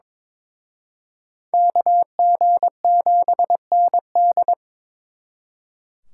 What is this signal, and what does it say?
Morse code, heard as a keyed tone, about 26 wpm, 700 Hz. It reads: KG7ND
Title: Speed Racing - USA Call Signs 22wpm